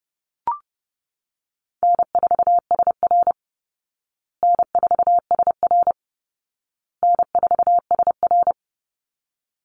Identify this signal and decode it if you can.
Morse code, heard as a keyed tone, about 28 wpm, 700 Hz. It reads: N4SR N4SR N4SR
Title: Call Signs Repeated 2x - 30wpm